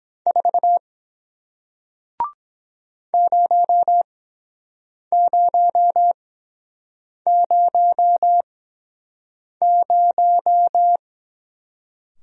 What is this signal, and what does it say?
Morse code, heard as a keyed tone, about 23 wpm, 700 Hz. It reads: 4 0 0 0 0